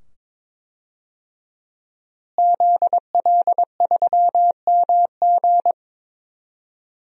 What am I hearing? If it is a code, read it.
Morse code, heard as a keyed tone, about 22 wpm, 700 Hz. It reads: ZL3MG
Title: DXCC-100 Call Signs - 5 Characters - Repeated 2x - 22wpm